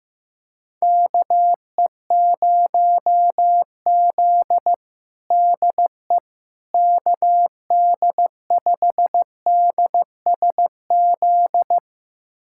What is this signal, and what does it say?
Morse code, heard as a keyed tone, about 15 wpm, 700 Hz. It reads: KE0Z DE KD5DSZ
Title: USA Callsign DE Callsign 15wpm